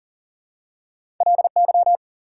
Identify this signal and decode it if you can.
Morse code, heard as a keyed tone, about 40 wpm, 700 Hz. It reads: LY
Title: Prefixes and Suffixes 40wpm